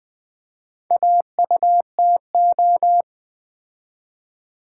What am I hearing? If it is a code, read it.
Morse code, heard as a keyed tone, about 20 wpm, 690 Hz. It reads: AUTO
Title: Build A Word - 2 Letters At A Time - 4-Letter Words - 20wpm